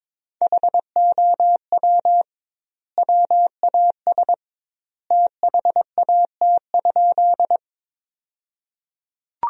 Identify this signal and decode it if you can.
Morse code, heard as a keyed tone, about 22 wpm, 700 Hz. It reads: HOW WAS THAT?
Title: 2x Word-Spacing - Sentences from Top 100 Words - 22wpm